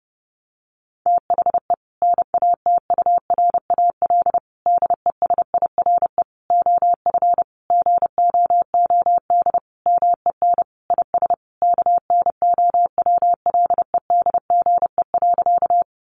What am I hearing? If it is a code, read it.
Morse code, heard as a keyed tone, about 30 wpm, 700 Hz. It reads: THE NATURAL DESIRE OF GOOD MEN IS KNOWLEDGE.
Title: Sentences from Top 1100 Words 30wpm